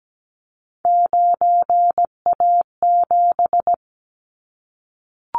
Morse code, heard as a keyed tone, about 17 wpm, 700 Hz. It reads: 9A7
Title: Callsign Prefixes for Top 100 DXCC - 17wpm